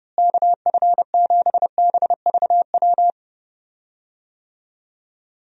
Morse code, heard as a keyed tone, about 29 wpm, 700 Hz. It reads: KF7BVW
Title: USA Call Signs - 6 Characters - Repeated 3x - 30wpm